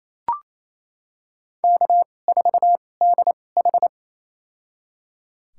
Morse code, heard as a keyed tone, about 28 wpm, spaced as for 20 wpm, 700 Hz. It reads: K4DH
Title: USA Call Signs - 4 Characters - 2x Character Spacing - 28wpm